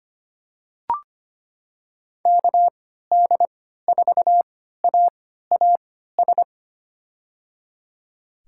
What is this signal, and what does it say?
Morse code, heard as a keyed tone, about 25 wpm, spaced as for 14 wpm, 700 Hz. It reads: KD4AAS
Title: USA Call Signs - 6 Characters - 3x Character Spacing - 25wpm